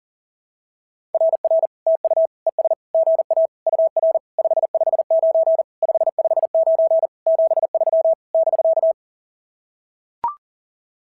Morse code, heard as a keyed tone, about 40 wpm, 650 Hz. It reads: RR TU ES GA UR 559 559 73 <BK>
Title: SOTA QSO Practice 40wpm